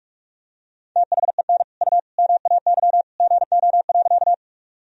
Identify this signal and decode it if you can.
Morse code, heard as a keyed tone, about 46 wpm, 700 Hz. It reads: THEN U MAY GO.
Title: Sentences from Top 100 Words and Abbreviations 45wpm